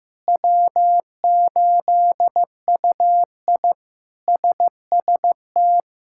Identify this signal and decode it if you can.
Morse code, heard as a keyed tone, about 15 wpm, 700 Hz. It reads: W8UI SST
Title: K1USN SST QSO Practice - 15wpm